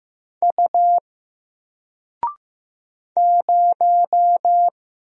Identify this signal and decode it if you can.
Morse code, heard as a keyed tone, about 15 wpm, 700 Hz. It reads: U 0
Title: Single Character (Letter:Number:Punctuation:Prosign) Alt 15wpm